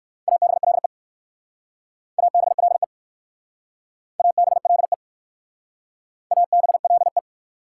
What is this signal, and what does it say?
Morse code, heard as a keyed tone, about 60 wpm, 690 Hz. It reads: ABLE ABLE ABLE ABLE